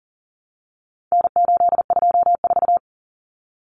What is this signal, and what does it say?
Morse code, heard as a keyed tone, about 40 wpm, 700 Hz. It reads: N824
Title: Common Cut Numbers - 4-Digit 40wpm